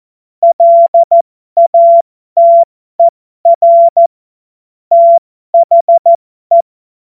Morse code, heard as a keyed tone, about 14 wpm, spaced as for 12 wpm, 680 Hz. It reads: LATER THE